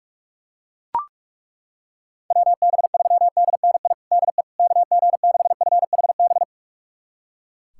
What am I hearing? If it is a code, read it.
Morse code, heard as a keyed tone, about 42 wpm, 700 Hz. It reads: WB3DNI DE KG6FHB
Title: USA Callsign DE Callsign - 6 Characters - 45wpm